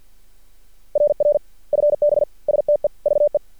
Morse code, heard as a keyed tone, about 50 wpm, 590 Hz. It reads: RR FB STEVE
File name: archived / CRSnet